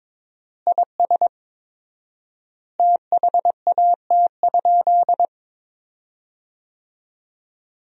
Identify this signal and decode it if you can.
Morse code, heard as a keyed tone, about 22 wpm, 700 Hz. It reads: IS THAT?
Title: 4x Word-Spacing - Sentences from Top 200 Words - 22wpm